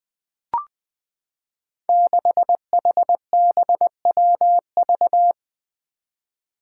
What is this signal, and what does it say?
Morse code, heard as a keyed tone, about 20 wpm, 700 Hz. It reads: 6HBWV
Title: Five Challenging Letter-Number Combinations 20wpm